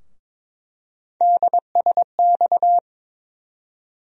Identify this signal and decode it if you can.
Morse code, heard as a keyed tone, about 22 wpm, 700 Hz. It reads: DSX
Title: USA Call Sign Postfixes - 3 Letters 22wpm